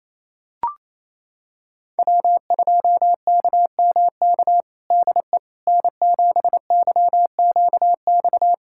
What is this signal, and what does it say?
Morse code, heard as a keyed tone, about 28 wpm, 700 Hz. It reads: W2KMK DE N7YQX
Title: USA Callsign DE Callsign - 5 Characters - 28wpm